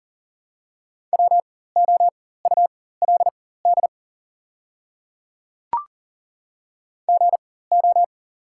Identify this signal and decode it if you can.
Morse code, heard as a keyed tone, about 40 wpm, spaced as for 19 wpm, 690 Hz. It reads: WOULD GO
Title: 4x Character Spacing - Top 100 Words - 40wpm